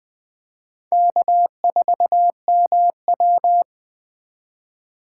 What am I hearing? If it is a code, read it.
Morse code, heard as a keyed tone, about 20 wpm, 700 Hz. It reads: K4MW
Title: USA Call Signs - 2x Character Spacing - 20wpm